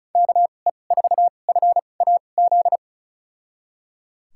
Morse code, heard as a keyed tone, about 36 wpm, spaced as for 25 wpm, 700 Hz. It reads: KE4FAZ
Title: USA Call Signs - 6 Characters - 2x Character Spacing - 35wpm